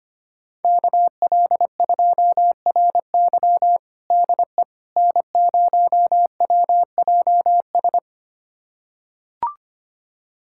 Morse code, heard as a keyed tone, about 25 wpm, 700 Hz. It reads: KL2RY DE N0WJS
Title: USA Callsign DE Callsign - 5 Characters - 25wpm